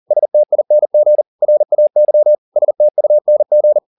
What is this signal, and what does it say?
Morse code, heard as a keyed tone, about 40 wpm, 600 Hz. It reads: STING RAY STUNG
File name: the old man and the sea 2 40wpm